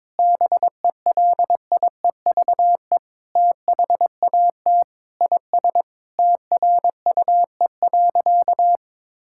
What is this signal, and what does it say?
Morse code, heard as a keyed tone, about 22 wpm, 700 Hz. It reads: BELIEVE THAT IS TRUE.
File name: practice / 22wpm / Sentences with 4-Word Phrases - 22wpm